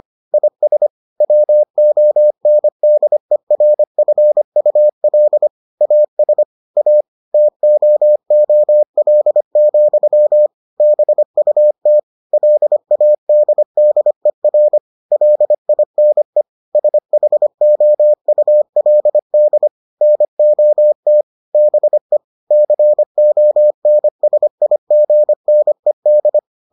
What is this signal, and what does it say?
Morse code, heard as a keyed tone, about 25 wpm, 600 Hz. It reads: IS WONDERFUL AS A TOOL, BUT LADDER LINE SHOULD NOT BE CONSIGNED